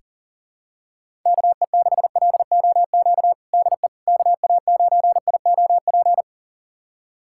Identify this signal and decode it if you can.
Morse code, heard as a keyed tone, about 40 wpm, 700 Hz. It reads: KE6LOQ DE KA9IOP
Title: USA Callsign DE Callsign 40wpm